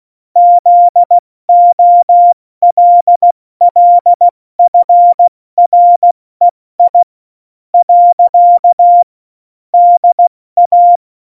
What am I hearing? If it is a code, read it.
Morse code, heard as a keyed tone, about 16 wpm, 700 Hz. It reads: ZOLLFREI . DA
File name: Morsekurs / Witze / Witz 04. Dezember, 70 F80